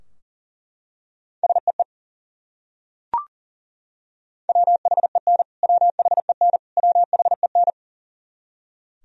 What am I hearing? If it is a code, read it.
Morse code, heard as a keyed tone, about 40 wpm, 700 Hz. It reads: SEE WHEN WHEN WHEN
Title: Top 100 Words Repeated 40wpm